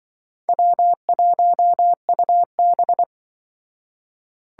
Morse code, heard as a keyed tone, about 24 wpm, 700 Hz. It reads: W1UB
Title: Speed Racing - USA Call Signs 20wpm